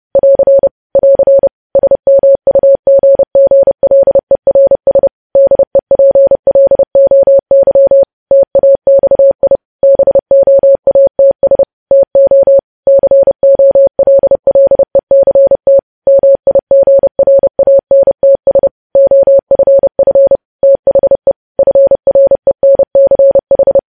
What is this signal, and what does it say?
Morse code, heard as a keyed tone, about 30 wpm, 550 Hz. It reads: <AR> <AR> SMUGGLERS DEPLOY TAXI BOATS TO COLLECT MIGRANTS OFF THE FRENCH
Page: news headlines in morse code at 30 wpm